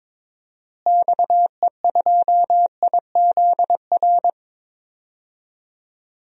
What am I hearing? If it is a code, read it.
Morse code, heard as a keyed tone, about 22 wpm, 700 Hz. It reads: XE2IZR
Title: DXCC-100 Call Signs - 6 Characters - Repeated 2x - 22wpm